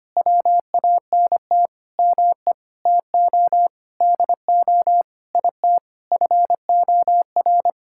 Morse code, heard as a keyed tone, about 25 wpm, 700 Hz. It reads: WANT ME TO DO IT FOR
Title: Sentences from Top 1000 Words - Review Part 6 - 25wpm